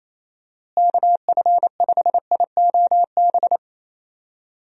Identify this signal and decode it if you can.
Morse code, heard as a keyed tone, about 28 wpm, 700 Hz. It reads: KF5IOB
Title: USA Call Signs - 6 Characters - Repeated 2x - 28wpm